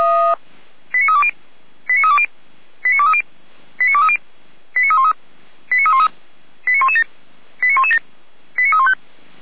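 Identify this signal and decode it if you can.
Five-tone selective call (ZVEI), heard as quick runs of five tones, tones of about 70 ms. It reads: 89320 89320 89230 89230 89313 89313 89117 89117 89315
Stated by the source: ZVEI five-tone selective call, received by SDR.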